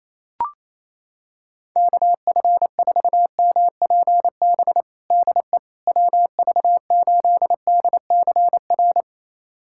Morse code, heard as a keyed tone, about 28 wpm, 700 Hz. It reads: KF4MPB DE WV8DCR
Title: USA Callsign DE Callsign 28wpm